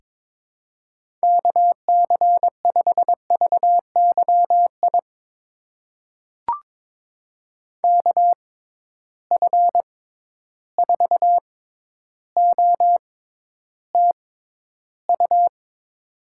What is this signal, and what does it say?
Morse code, heard as a keyed tone, about 22 wpm, spaced as for 11 wpm, 700 Hz. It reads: KC5VYI KF4OTU
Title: USA Call Signs - 6 Characters - 6x Character Spacing - 22wpm